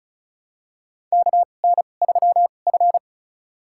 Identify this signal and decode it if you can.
Morse code, heard as a keyed tone, about 34 wpm, spaced as for 25 wpm, 700 Hz. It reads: KN3F
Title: USA Call Signs - 2x Character Spacing - 35wpm